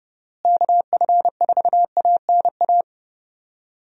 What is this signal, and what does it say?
Morse code, heard as a keyed tone, about 30 wpm, 700 Hz. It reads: KF4ANA
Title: USA Call Signs - 6 Characters - Repeated 3x - 30wpm